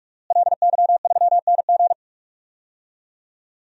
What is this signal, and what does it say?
Morse code, heard as a keyed tone, about 44 wpm, 700 Hz. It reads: LY3NG
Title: DXCC-100 Call Signs - 5 Characters - Repeated 2x - 45wpm